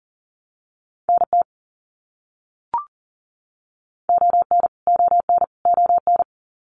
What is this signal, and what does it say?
Morse code, heard as a keyed tone, about 40 wpm, 700 Hz. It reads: NT ON ON ON
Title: Canadian Provinces and Territories - Repeated 3x - 40wpm